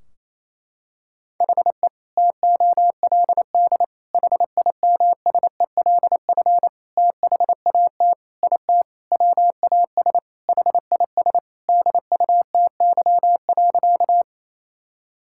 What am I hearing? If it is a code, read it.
Morse code, heard as a keyed tone, about 28 wpm, 700 Hz. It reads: HE TOLD HIMSELF THAT IT WAS HIS DUTY.
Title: Sentences from Top 1600 Words 28wpm